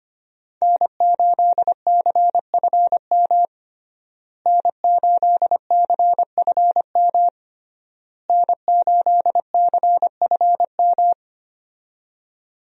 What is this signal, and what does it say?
Morse code, heard as a keyed tone, about 25 wpm, 700 Hz. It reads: N8CFM N8CFM N8CFM
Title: USA Call Signs - 5 Characters - Repeated 3x - 25wpm